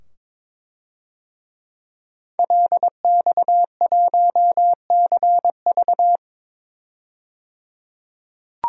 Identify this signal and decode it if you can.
Morse code, heard as a keyed tone, about 22 wpm, 700 Hz. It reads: LX1CV
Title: DXCC-100 Call Signs - 5 Characters - 22wpm